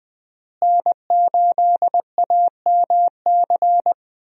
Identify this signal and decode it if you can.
Morse code, heard as a keyed tone, about 20 wpm, 700 Hz. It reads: N8AMC
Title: USA Call Signs - 5 Characters - Repeated 3x - 20wpm